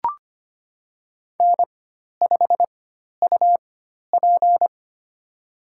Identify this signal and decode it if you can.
Morse code, heard as a keyed tone, about 25 wpm, spaced as for 12 wpm, 700 Hz. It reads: N5UP
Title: USA Call Signs - 4 Characters - 4x Character Spacing - 25wpm